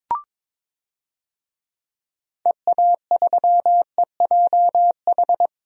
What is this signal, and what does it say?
Morse code, heard as a keyed tone, about 22 wpm, 700 Hz. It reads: EA3EJH
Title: DXCC-100 Call Signs - 6 Characters - 22wpm